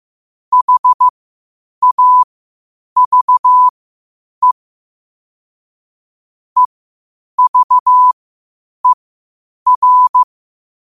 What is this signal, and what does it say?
Morse code, heard as a keyed tone, about 15 wpm, spaced as for 8 wpm, 1000 Hz. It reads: HAVE EVER